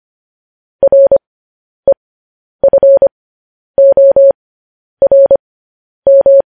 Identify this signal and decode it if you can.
Morse code, heard as a keyed tone, about 25 wpm, spaced as for 10 wpm, 550 Hz. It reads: REFORM
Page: news-headlines-in-morse-code-at-10-wpm